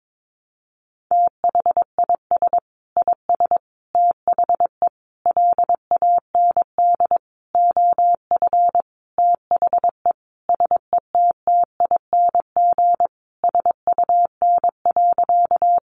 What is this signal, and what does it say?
Morse code, heard as a keyed tone, about 22 wpm, 700 Hz. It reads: THIS IS THE LAND OF THE SETTING SUN.
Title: Sentences from Top 1300 Words 22wpm